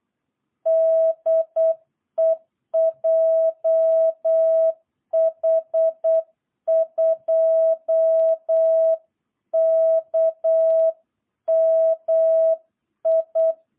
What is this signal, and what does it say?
Morse code, heard as a keyed tone, about 8 wpm, 660 Hz. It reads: DE JH2KMI
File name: jh8cbh / 2022cw